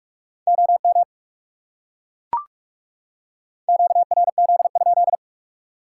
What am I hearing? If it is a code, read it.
Morse code, heard as a keyed tone, about 40 wpm, 700 Hz. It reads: OM QRZ?